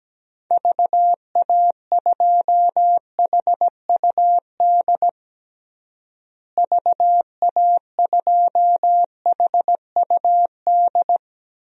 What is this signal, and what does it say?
Morse code, heard as a keyed tone, about 17 wpm, 700 Hz. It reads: VA2HUD VA2HUD
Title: DXCC-100 Call Signs - 6 Characters - Repeated 2x - 17wpm